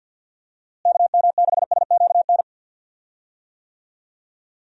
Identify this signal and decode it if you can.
Morse code, heard as a keyed tone, about 53 wpm, 700 Hz. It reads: KM6SQN
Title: USA Call Signs - 6 Characters - Repeated 2x - 50wpm